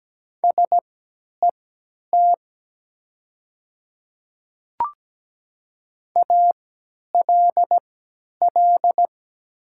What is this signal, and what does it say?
Morse code, heard as a keyed tone, about 17 wpm, spaced as for 10 wpm, 700 Hz. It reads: SET ALL
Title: 3x Character Spacing - Top 200 Words - 17wpm